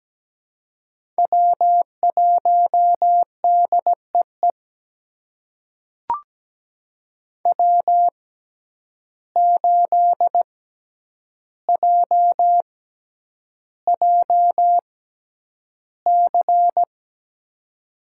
Morse code, heard as a keyed tone, about 17 wpm, spaced as for 9 wpm, 700 Hz. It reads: W1DEE W8JJC
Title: USA Call Signs - 5 Characters - 6x Character Spacing - 17wpm